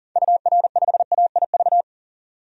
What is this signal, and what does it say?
Morse code, heard as a keyed tone, about 37 wpm, 700 Hz. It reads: UR5AIV
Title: DXCC-100 Call Signs - 6 Characters - 40wpm